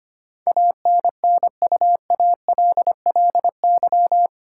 Morse code, heard as a keyed tone, about 25 wpm, 700 Hz. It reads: ANNUALLY
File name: 8 Letter Words - First 3 Letters Given - 25wpm